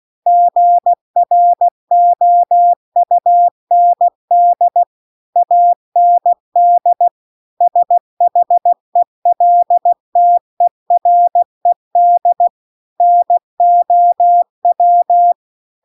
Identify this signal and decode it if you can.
Morse code, heard as a keyed tone, about 16 wpm, 700 Hz. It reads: GROUND AND SHELTERED NOW